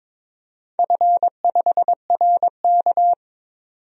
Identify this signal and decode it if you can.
Morse code, heard as a keyed tone, about 22 wpm, 700 Hz. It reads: F5RK
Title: DXCC-100 Call Signs - 4 Characters - 22wpm